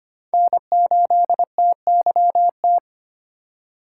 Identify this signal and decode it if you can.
Morse code, heard as a keyed tone, about 25 wpm, 700 Hz. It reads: N8TYT
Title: USA Call Signs - 5 Characters - 3x Character Spacing - 25wpm